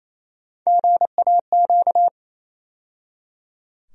Morse code, heard as a keyed tone, about 28 wpm, 700 Hz. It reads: GAQ
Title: USA Call Sign Postfixes - 3 Letters 28wpm